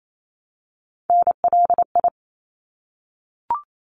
Morse code, heard as a keyed tone, about 28 wpm, 700 Hz. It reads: NLI
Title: ARRL Sections 28wpm